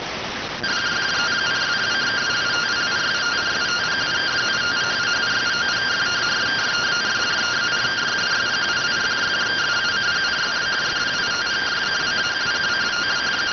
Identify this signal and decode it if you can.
Two-tone teleprinter signal, about 170 Hz shift.